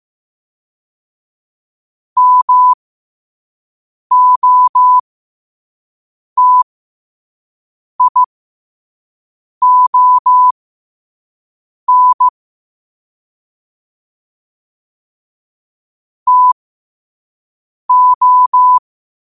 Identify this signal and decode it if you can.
Morse code, heard as a keyed tone, about 15 wpm, spaced as for 5 wpm, 1000 Hz. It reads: MOTION TO
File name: qotdfiles / 5wpm